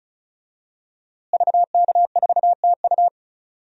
Morse code, heard as a keyed tone, about 34 wpm, 700 Hz. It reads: VK4TU
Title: DXCC-100 Call Signs - 5 Characters - Repeated 3x - 35wpm